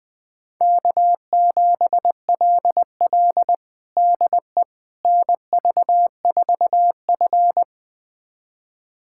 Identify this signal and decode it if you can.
Morse code, heard as a keyed tone, about 20 wpm, 700 Hz. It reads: K7LL DE NV4F
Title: USA Callsign DE Callsign - 4 Characters - 20wpm